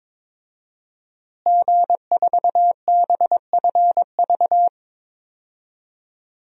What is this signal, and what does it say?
Morse code, heard as a keyed tone, about 22 wpm, 700 Hz. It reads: G4BFV
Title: DXCC-100 Call Signs - 5 Characters - Repeated 2x - 22wpm